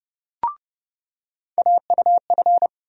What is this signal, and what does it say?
Morse code, heard as a keyed tone, about 30 wpm, 700 Hz. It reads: AUF
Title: USA Call Sign Postfixes - 3 Letters 30wpm